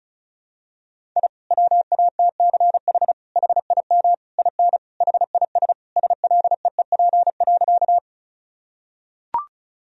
Morse code, heard as a keyed tone, about 34 wpm, 700 Hz. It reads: I WATCH HIM IN HIS SLEEP.
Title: Sentences from Top 1000 Words - Review Part 1 - 35wpm